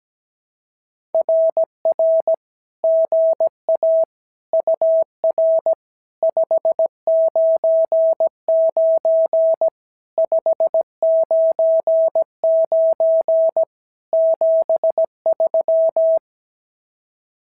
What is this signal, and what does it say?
Morse code, heard as a keyed tone, about 16 wpm, 650 Hz. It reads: RR GA UR 599 599 73
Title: SOTA QSO Practice 17wpm